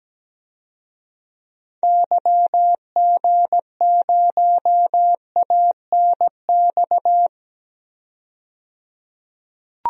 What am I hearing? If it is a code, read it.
Morse code, heard as a keyed tone, about 17 wpm, 700 Hz. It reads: YG0ANX
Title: DXCC-100 Call Signs - 6 Characters - 17wpm